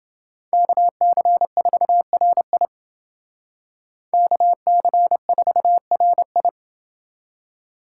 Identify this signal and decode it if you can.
Morse code, heard as a keyed tone, about 28 wpm, 700 Hz. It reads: KC4RI KC4RI